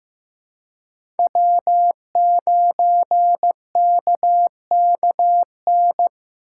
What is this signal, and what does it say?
Morse code, heard as a keyed tone, about 15 wpm, 700 Hz. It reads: W9KKN
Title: USA Call Signs - 5 Characters - 4x Character Spacing - 15wpm